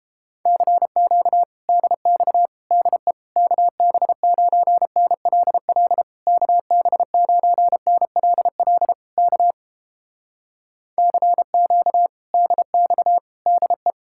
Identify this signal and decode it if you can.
Morse code, heard as a keyed tone, about 31 wpm, 700 Hz. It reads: CQ DX DE KB9NLL KB9NLL K CQ DX DE